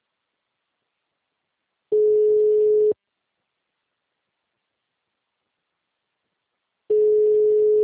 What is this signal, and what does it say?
Ringback tone (Europe, 425 Hz).
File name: out- -2017- - - .313